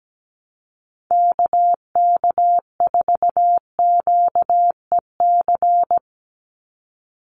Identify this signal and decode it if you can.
Morse code, heard as a keyed tone, about 17 wpm, 700 Hz. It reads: KK4QEC
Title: USA Call Signs 17wpm